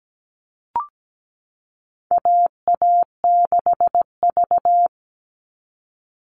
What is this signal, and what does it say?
Morse code, heard as a keyed tone, about 17 wpm, 700 Hz. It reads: AA6V
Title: All Cut Numbers - 4-Digit 17wpm